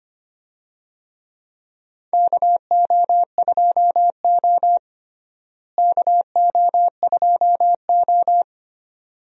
Morse code, heard as a keyed tone, about 25 wpm, 700 Hz. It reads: KO2O KO2O
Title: USA Call Signs - 4 Characters - Repeated 2x - 25wpm